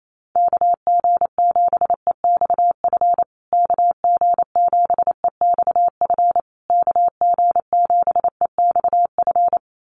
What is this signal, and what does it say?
Morse code, heard as a keyed tone, about 28 wpm, 700 Hz. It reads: KG7EXF KG7EXF KG7EXF
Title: USA Call Signs Repeated 3x 28wpm